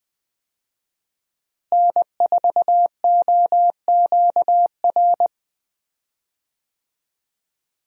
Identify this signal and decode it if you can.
Morse code, heard as a keyed tone, about 20 wpm, 700 Hz. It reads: N4OQR
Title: USA Call Signs - 5 Characters - Repeated 2x - 20wpm